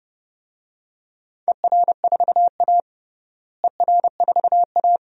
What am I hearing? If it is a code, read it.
Morse code, heard as a keyed tone, about 30 wpm, 700 Hz. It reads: ER4A ER4A
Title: DXCC-100 Call Signs - 4 Characters - Repeated 2x - 30wpm